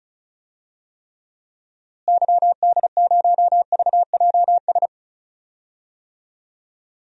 Morse code, heard as a keyed tone, about 36 wpm, 700 Hz. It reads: YD0VJS
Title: DXCC-100 Call Signs - 6 Characters - Repeated 2x - 35wpm